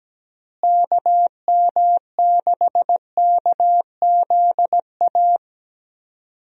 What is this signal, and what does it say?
Morse code, heard as a keyed tone, about 17 wpm, 700 Hz. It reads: KM6KZA
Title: USA Call Signs - 6 Characters - Repeated 2x - 17wpm